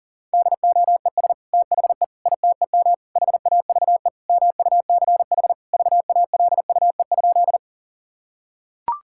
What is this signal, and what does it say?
Morse code, heard as a keyed tone, about 40 wpm, 700 Hz. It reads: DOES THE ITEM HAVE MUCH VALUE?
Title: Sentences from Top 1000 Words - Review Part 6 - 40wpm